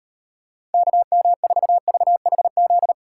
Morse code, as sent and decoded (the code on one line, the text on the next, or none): -.- -- ....- ...- .... --..
KM4VHZ